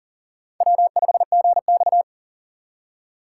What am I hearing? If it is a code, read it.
Morse code, heard as a keyed tone, about 40 wpm, 700 Hz. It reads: W5GX
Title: USA Call Signs - 4 Characters - 3x Character Spacing - 40wpm